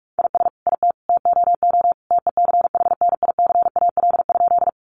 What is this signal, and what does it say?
Morse code, heard as a keyed tone, about 44 wpm, 700 Hz. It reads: IS IT TOO TECHNICAL?
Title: Sentences from Top 1100 Words 45wpm